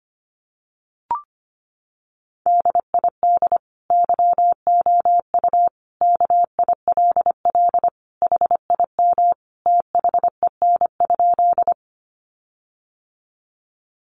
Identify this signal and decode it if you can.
Morse code, heard as a keyed tone, about 25 wpm, 700 Hz. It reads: DID YOU KILL HIM THEN?
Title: Sentences from Top 1400 Words 25wpm